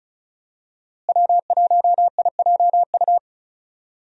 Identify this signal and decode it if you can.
Morse code, heard as a keyed tone, about 36 wpm, 700 Hz. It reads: W1IJU
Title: USA Call Signs - 5 Characters - Repeated 3x - 35wpm